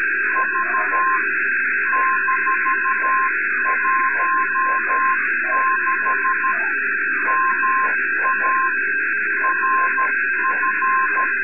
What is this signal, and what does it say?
Morse code, heard as a keyed tone, about 13 wpm, 1050 Hz, with noise. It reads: V -16 MA RF
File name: sat / RADIOSCAF-B 24Dec2011